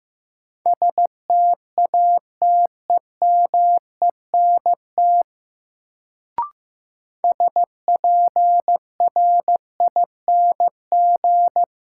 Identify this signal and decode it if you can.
Morse code, heard as a keyed tone, about 16 wpm, 700 Hz. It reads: STATEMENT SPRING
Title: Sentences from Top 1200 Words - New Word Drill Alt - 15wpm